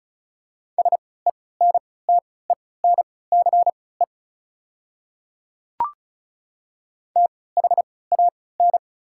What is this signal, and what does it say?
Morse code, heard as a keyed tone, about 34 wpm, spaced as for 20 wpm, 690 Hz. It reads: SENTENCE THAN
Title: 3x Character Spacing - Top 300 Words - 35wpm